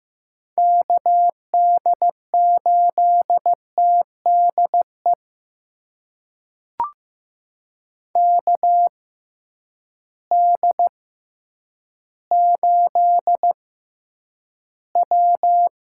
Morse code, heard as a keyed tone, about 15 wpm, spaced as for 9 wpm, 700 Hz. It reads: KD8TDE KD8W
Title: USA Call Signs - 6 Characters - 6x Character Spacing - 15wpm